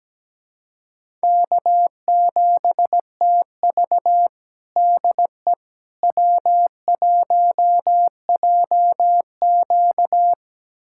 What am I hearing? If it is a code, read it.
Morse code, heard as a keyed tone, about 17 wpm, 700 Hz. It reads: K7TV DE W1JQ
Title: USA Callsign DE Callsign - 4 Characters - 17wpm